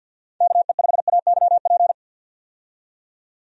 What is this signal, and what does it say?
Morse code, heard as a keyed tone, about 50 wpm, 700 Hz. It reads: KE5AYP
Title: USA Call Signs - 6 Characters - 50wpm